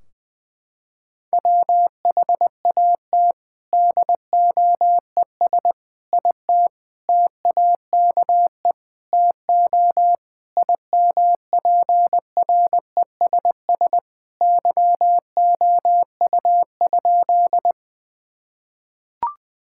Morse code, heard as a keyed tone, about 20 wpm, 700 Hz. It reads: WHAT DOES IT TAKE TO IMPRESS YOU?
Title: Sentences from Top 1600 Words 20wpm